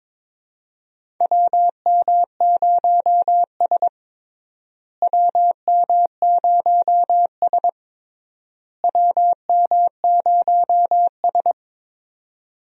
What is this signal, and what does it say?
Morse code, heard as a keyed tone, about 21 wpm, 700 Hz. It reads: WM0S WM0S WM0S
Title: USA Call Signs - 4 Characters - Repeated 3x - 22wpm